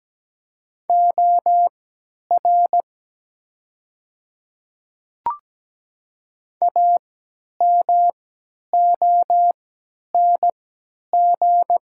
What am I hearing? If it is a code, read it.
Morse code, heard as a keyed tone, about 17 wpm, spaced as for 10 wpm, 700 Hz. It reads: OR AMONG
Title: 3x Character Spacing - Top 500 Words - 17wpm